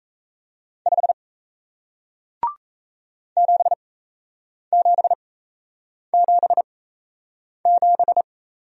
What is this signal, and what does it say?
Morse code, heard as a keyed tone, about 38 wpm, spaced as for 8 wpm, 700 Hz. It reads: 5 7777